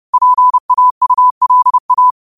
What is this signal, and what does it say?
Morse code, heard as a keyed tone, about 30 wpm, 1000 Hz. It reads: PAULA
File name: qotdfiles / 30wpm